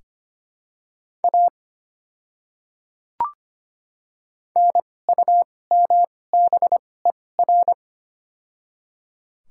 Morse code, heard as a keyed tone, about 25 wpm, spaced as for 18 wpm, 700 Hz. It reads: A NUMBER
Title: 2x Character Spacing - Top 300 Words - 25wpm